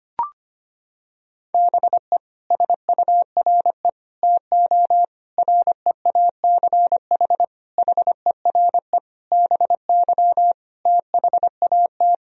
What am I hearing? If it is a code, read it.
Morse code, heard as a keyed tone, about 25 wpm, 700 Hz. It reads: BE SURE TO REACH HERE BY THAT
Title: Sentences from Top 1000 Words - Review Part 2 - 25wpm